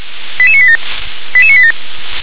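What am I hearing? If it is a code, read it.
Five-tone selective call (ZVEI), heard as quick runs of five tones, tones of about 70 ms. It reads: 88987 88987
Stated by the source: ZVEI five-tone call received by SDR.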